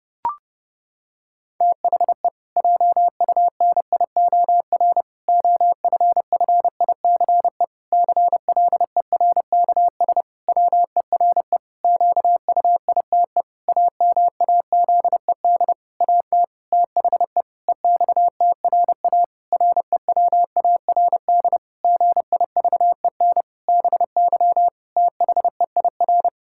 Morse code, heard as a keyed tone, about 30 wpm, 700 Hz. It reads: THE JUNIOR OFFICE CLERKS WERE QUITE AMAZED AT THE EXTRA REWARD GIVEN BY THEIR
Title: Pangram Challenge v2 - 30wpm